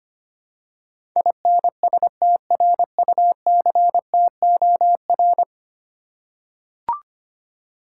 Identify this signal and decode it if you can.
Morse code, heard as a keyed tone, about 25 wpm, 700 Hz. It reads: INSTRUCTOR
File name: 10 Letter Words - First 3 Letters Given - 25wpm